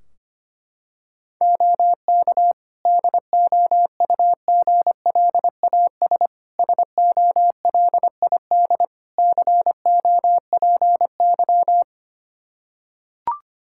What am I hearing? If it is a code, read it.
Morse code, heard as a keyed tone, about 25 wpm, 700 Hz. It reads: OK DOUGLAS SOLID COPY
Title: Practice QSO - Exchange 3 - More Info - 25wpm